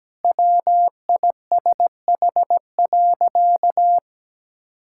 Morse code, heard as a keyed tone, about 17 wpm, 700 Hz. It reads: WISH.
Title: Sentences from Top 1600 Words 17wpm